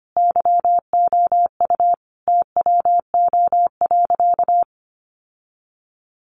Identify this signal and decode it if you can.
Morse code, heard as a keyed tone, about 25 wpm, 700 Hz. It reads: YOU TWO.
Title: Sentences from Top 1200 Words 25wpm